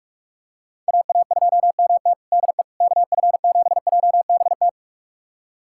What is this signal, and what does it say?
Morse code, heard as a keyed tone, about 46 wpm, 700 Hz. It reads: AA2MT DE KF7JBT
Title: USA Callsign DE Callsign 45wpm